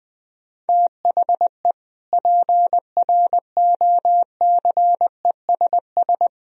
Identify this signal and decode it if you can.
Morse code, heard as a keyed tone, about 20 wpm, 700 Hz. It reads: THE PROCESS
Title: Sentences from Top 1000 Words - Review Part 3 - 20wpm